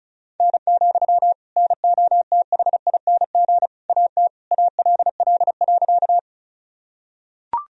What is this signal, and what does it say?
Morse code, heard as a keyed tone, about 36 wpm, 700 Hz. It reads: N, NOTHING AT ALL.
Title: Sentences from Top 500 Words and Abbreviations 35wpm